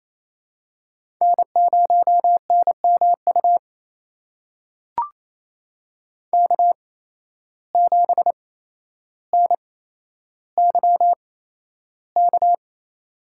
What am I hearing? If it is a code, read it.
Morse code, heard as a keyed tone, about 28 wpm, spaced as for 12 wpm, 700 Hz. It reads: N0NMU K7NYK
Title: USA Call Signs - 5 Characters - 8x Character Spacing - 28wpm